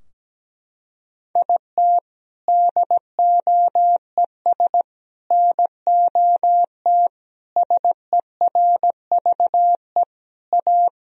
Morse code, heard as a keyed tone, about 17 wpm, 700 Hz. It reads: IT DOES NOT SERVE A
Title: Sentences from Top 1000 Words - Review Part 2 - 17wpm